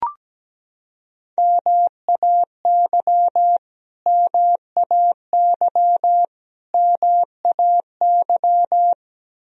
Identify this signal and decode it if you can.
Morse code, heard as a keyed tone, about 17 wpm, 700 Hz. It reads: MAY MAY MAY
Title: Top 100 Words Repeated 17wpm